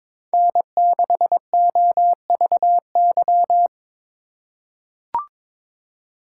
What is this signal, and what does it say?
Morse code, heard as a keyed tone, about 21 wpm, 700 Hz. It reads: N6OVY
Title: USA Call Signs - 5 Characters - 4x Character Spacing - 22wpm